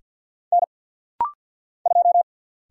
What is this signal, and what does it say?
Morse code, heard as a keyed tone, about 50 wpm, 700 Hz. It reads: N 2
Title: Single Letter-Number - Rapid-Fire 50wpm